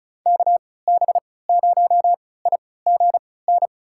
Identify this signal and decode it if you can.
Morse code, heard as a keyed tone, about 34 wpm, spaced as for 20 wpm, 700 Hz. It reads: KB0IGN
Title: USA Call Signs - 6 Characters - 3x Character Spacing - 35wpm